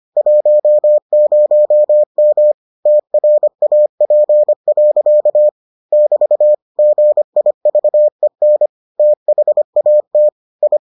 Morse code, heard as a keyed tone, about 25 wpm, 600 Hz. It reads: 10M TRAP. <BT> GIVEN THAT I